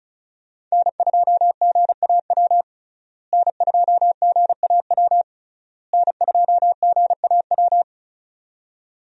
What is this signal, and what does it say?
Morse code, heard as a keyed tone, about 34 wpm, 700 Hz. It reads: N2GAW N2GAW N2GAW
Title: USA Call Signs - 5 Characters - Repeated 3x - 35wpm